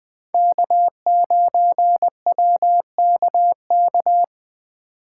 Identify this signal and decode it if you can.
Morse code, heard as a keyed tone, about 24 wpm, 700 Hz. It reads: K9WKK
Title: Speed Racing - USA Call Signs 20wpm